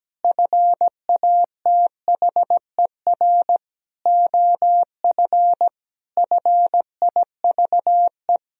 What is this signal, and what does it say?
Morse code, heard as a keyed tone, about 17 wpm, 700 Hz. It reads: FATHER OF FIVE
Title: Sentences from Top 1000 Words - Review Part 2 - 17wpm